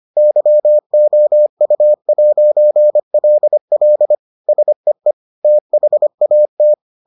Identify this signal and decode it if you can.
Morse code, heard as a keyed tone, about 25 wpm, 600 Hz. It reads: YOU'LL SEE THAT